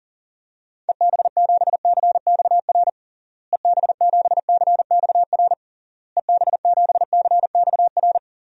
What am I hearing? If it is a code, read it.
Morse code, heard as a keyed tone, about 40 wpm, 700 Hz. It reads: EB7CXR EB7CXR EB7CXR
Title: DXCC-100 Call Signs - 6 Characters - Repeated 3x - 40wpm